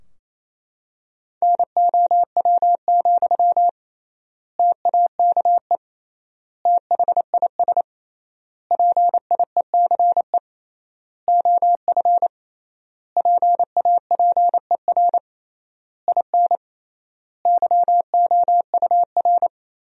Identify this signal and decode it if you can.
Morse code, heard as a keyed tone, about 28 wpm, 700 Hz. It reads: NOW, TAKE THIS PIECE OF PAPER IN YOUR
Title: 3x Word-Spacing - Sentences from Top 400 Words - 28wpm